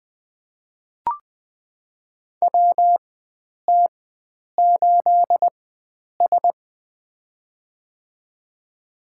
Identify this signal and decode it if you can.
Morse code, heard as a keyed tone, about 21 wpm, spaced as for 9 wpm, 700 Hz. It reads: WT8S
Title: USA Call Signs - 4 Characters - 4x Character Spacing - 20wpm